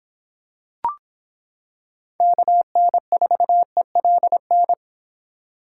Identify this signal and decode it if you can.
Morse code, heard as a keyed tone, about 26 wpm, 700 Hz. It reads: KN4ELN